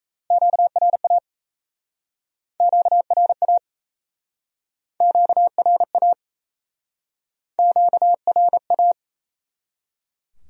Morse code, heard as a keyed tone, about 39 wpm, 700 Hz. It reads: QRA QRA QRA QRA